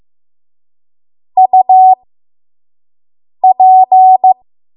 Morse code, heard as a keyed tone, about 15 wpm, spaced as for 5 wpm, 750 Hz. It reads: UP
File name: Archive / 5 WPM / 05WPM